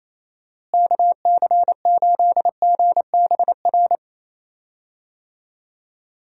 Morse code, heard as a keyed tone, about 28 wpm, 700 Hz. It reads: KC8GBR
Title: USA Call Signs - 6 Characters - Repeated 3x - 28wpm